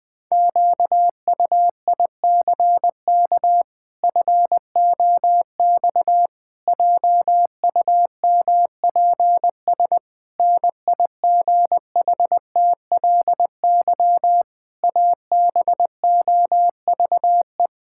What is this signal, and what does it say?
Morse code, heard as a keyed tone, about 20 wpm, 700 Hz. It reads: QUICK FOX JUMPS NIGHTLY ABOVE
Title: Pangram Challenge v2 - 20wpm